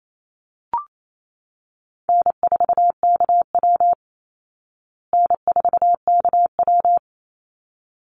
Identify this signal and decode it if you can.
Morse code, heard as a keyed tone, about 28 wpm, 700 Hz. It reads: N4KW N4KW
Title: Easy Call Signs Repeated 2x - 28wpm